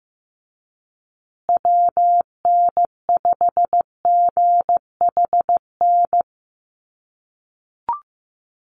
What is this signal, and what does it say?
Morse code, heard as a keyed tone, about 15 wpm, 700 Hz. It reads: WN5GHN
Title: USA Call Signs Repeated 2x 15wpm